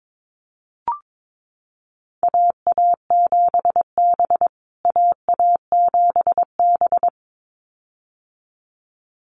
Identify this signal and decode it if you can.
Morse code, heard as a keyed tone, about 22 wpm, 700 Hz. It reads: AA7B AA7B
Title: USA Call Signs Repeated 2x 22wpm